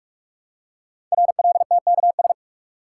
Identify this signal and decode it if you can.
Morse code, heard as a keyed tone, about 44 wpm, 700 Hz. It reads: RLTKS
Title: Code Groups - 5 Letters - 45wpm